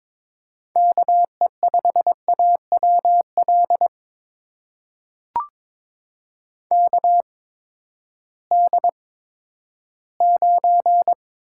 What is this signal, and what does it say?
Morse code, heard as a keyed tone, about 22 wpm, spaced as for 12 wpm, 700 Hz. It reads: KE5AWL KD9
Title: USA Call Signs - 6 Characters - 8x Character Spacing - 22wpm